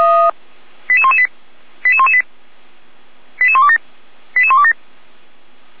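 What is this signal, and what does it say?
Five-tone selective call (ZVEI), heard as quick runs of five tones, tones of about 70 ms. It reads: 88118 88118 88127 88127
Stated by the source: ZVEI five-tone call received by SDR, from the Feuerwehr Orlamünde audio archive.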